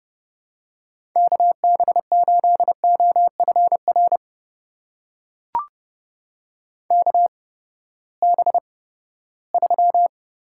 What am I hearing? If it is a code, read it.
Morse code, heard as a keyed tone, about 32 wpm, spaced as for 17 wpm, 700 Hz. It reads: KB8OFR KB3
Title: USA Call Signs - 6 Characters - 8x Character Spacing - 30wpm